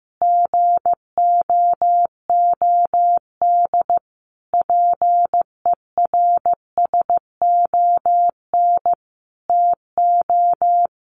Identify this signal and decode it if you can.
Morse code, heard as a keyed tone, about 15 wpm, 700 Hz. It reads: GOOD PERSON TO
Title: Sentences from Top 1400 Words 15wpm